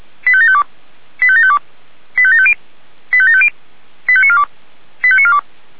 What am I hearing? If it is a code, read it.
Five-tone selective call (ZVEI), heard as quick runs of five tones, tones of about 70 ms. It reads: 85752 85752 85750 85750 85942 85942